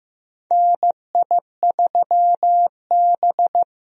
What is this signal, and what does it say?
Morse code, heard as a keyed tone, about 15 wpm, 700 Hz. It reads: NI3B
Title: USA Call Signs - 4 Characters - Repeated 2x - 15wpm